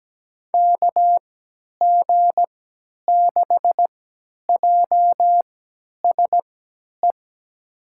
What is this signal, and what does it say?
Morse code, heard as a keyed tone, about 17 wpm, spaced as for 10 wpm, 700 Hz. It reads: KG6JSE
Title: USA Call Signs - 6 Characters - 3x Character Spacing - 17wpm